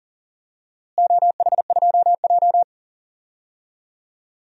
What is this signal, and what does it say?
Morse code, heard as a keyed tone, about 40 wpm, 700 Hz. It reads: OH2J
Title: DXCC-100 Call Signs - 4 Characters - Repeated 2x - 40wpm